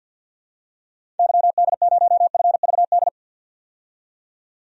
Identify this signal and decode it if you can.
Morse code, heard as a keyed tone, about 50 wpm, 700 Hz. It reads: YD0FVD
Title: DXCC-100 Call Signs - 6 Characters - 50wpm